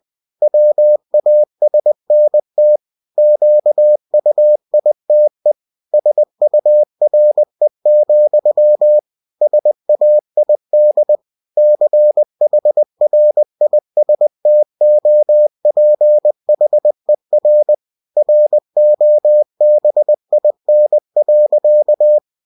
Morse code, heard as a keyed tone, about 20 wpm, 600 Hz. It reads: WASNT QUITE SURE, SAID CHRISTOPHER ROBIN.
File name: cw-books / winnie-the-pooh / winnie-20wpm-page-0002